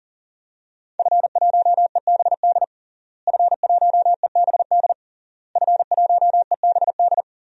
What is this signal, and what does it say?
Morse code, heard as a keyed tone, about 37 wpm, 700 Hz. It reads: F1EBD F1EBD F1EBD
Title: DXCC-100 Call Signs - 5 Characters - Repeated 3x - 40wpm